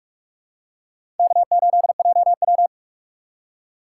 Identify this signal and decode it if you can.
Morse code, heard as a keyed tone, about 44 wpm, 690 Hz. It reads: K8JW
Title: USA Call Signs - 4 Characters - 3x Character Spacing - 45wpm